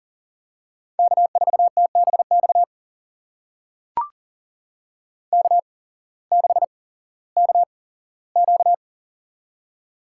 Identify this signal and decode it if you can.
Morse code, heard as a keyed tone, about 40 wpm, spaced as for 19 wpm, 700 Hz. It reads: K4TBX K6KQ
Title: USA Call Signs - 8x Character Spacing - 40wpm